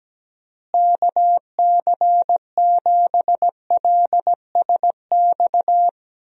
Morse code, heard as a keyed tone, about 17 wpm, 700 Hz. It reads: KC7LSX
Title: USA Call Signs - 6 Characters - Repeated 3x - 17wpm